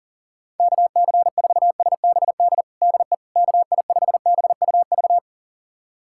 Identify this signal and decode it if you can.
Morse code, heard as a keyed tone, about 40 wpm, 700 Hz. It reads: KC4SBD DE KI5BUV
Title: USA Callsign DE Callsign - 6 Characters - 40wpm